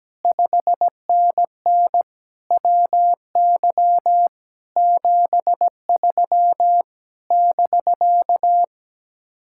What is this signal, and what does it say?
Morse code, heard as a keyed tone, about 17 wpm, 700 Hz. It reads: 5NN WY 73 <BK>
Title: Speed-Racing Common QSO Phrases Encore - 17wpm